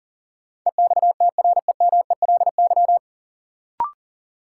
Morse code, heard as a keyed tone, about 40 wpm, 700 Hz. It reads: EXTREMELY
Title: Sentences from Top 1000 Words - Review Part 7 - New Word Drill - 40wpm